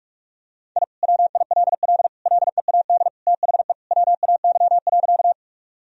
Morse code, heard as a keyed tone, about 41 wpm, 700 Hz. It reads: I WILL LEAD THE WAY.
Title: Sentences from Top 1000 Words - Review Part 2 - 45wpm